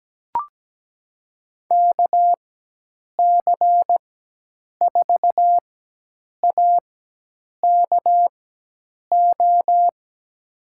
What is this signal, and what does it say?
Morse code, heard as a keyed tone, about 17 wpm, spaced as for 8 wpm, 700 Hz. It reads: KC4AKO
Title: USA Call Signs - 6 Characters - 4x Character Spacing - 17wpm